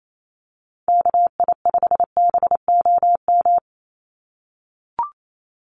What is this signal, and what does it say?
Morse code, heard as a keyed tone, about 28 wpm, 700 Hz. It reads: KI5BOM
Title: USA Call Signs 28wpm